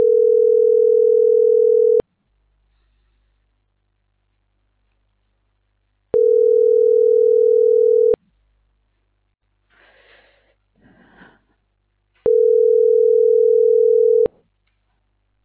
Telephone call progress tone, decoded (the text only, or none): ringback tone